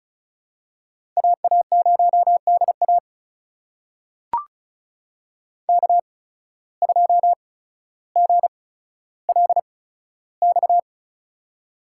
Morse code, heard as a keyed tone, about 36 wpm, spaced as for 15 wpm, 700 Hz. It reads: AA0DA K2GLX
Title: USA Call Signs - 5 Characters - 8x Character Spacing - 35wpm